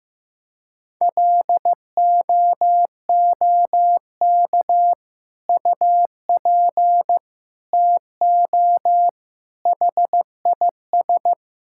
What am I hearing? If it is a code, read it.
Morse code, heard as a keyed tone, about 15 wpm, 700 Hz. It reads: LOOK UP TO HIS
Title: Sentences from Top 1600 Words 15wpm